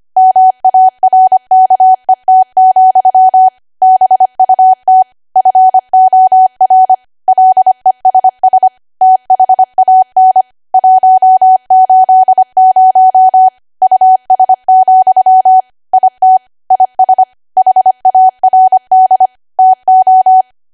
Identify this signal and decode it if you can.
Morse code, heard as a keyed tone, about 25 wpm, 750 Hz. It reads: MARKET, BUT FOR LESS THAN 180 US, IT IS HARD TO